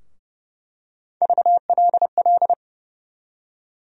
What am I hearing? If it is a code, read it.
Morse code, heard as a keyed tone, about 31 wpm, 700 Hz. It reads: VLL
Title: USA Call Sign Postfixes - 3 Letters 30wpm